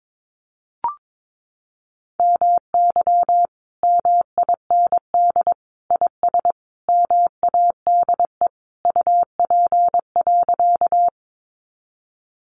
Morse code, heard as a keyed tone, about 22 wpm, 700 Hz. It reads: MY MIND IS MADE UP.
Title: Sentences from Top 500 Words 22wpm